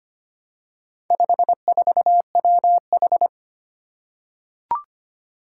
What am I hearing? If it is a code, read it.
Morse code, heard as a keyed tone, about 25 wpm, 700 Hz. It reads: 54WH
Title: Four Challenging Letter-Number Combinations 25wpm